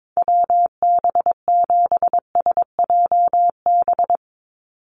Morse code, heard as a keyed tone, about 22 wpm, 700 Hz. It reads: WB7SJB
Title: USA Call Signs 22wpm